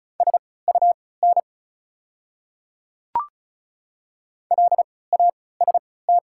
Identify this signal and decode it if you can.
Morse code, heard as a keyed tone, about 36 wpm, spaced as for 20 wpm, 700 Hz. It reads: SUN LAST
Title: 3x Character Spacing - Top 300 Words - 35wpm